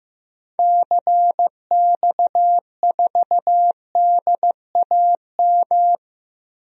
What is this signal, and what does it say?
Morse code, heard as a keyed tone, about 15 wpm, 700 Hz. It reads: CX4DAM
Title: DXCC-100 Call Signs - 6 Characters - Repeated 2x - 15wpm